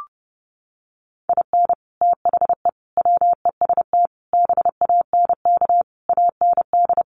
Morse code, heard as a keyed tone, about 30 wpm, 700 Hz. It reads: IN THE WEST BANK AND
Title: Sets of 5 Words Encore2 30wpm